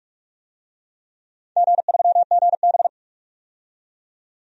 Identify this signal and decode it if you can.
Morse code, heard as a keyed tone, about 46 wpm, 700 Hz. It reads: G3GB
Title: DXCC-100 Call Signs - 4 Characters - 45wpm